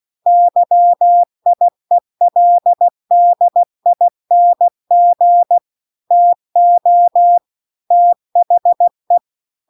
Morse code, heard as a keyed tone, about 16 wpm, 700 Hz. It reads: YIELDING TO THE